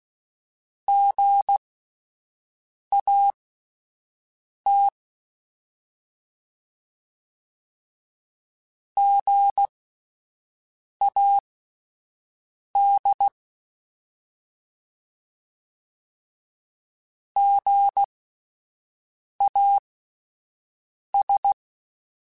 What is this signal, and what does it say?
Morse code, heard as a keyed tone, about 16 wpm, spaced as for 5 wpm, 800 Hz. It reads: GAT GAD GAS
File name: Alphabetical / Alphabetical mp3s / Gs @5